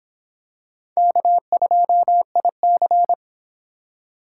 Morse code, heard as a keyed tone, about 25 wpm, 700 Hz. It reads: K2IC